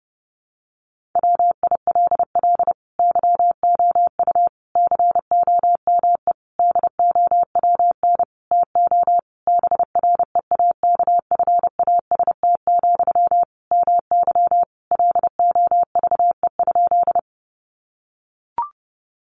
Morse code, heard as a keyed tone, about 30 wpm, 700 Hz. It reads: WILL YOU COME DOWN TO BREAKFAST, MY LOVE?
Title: Sentences from Top 1500 Words 30wpm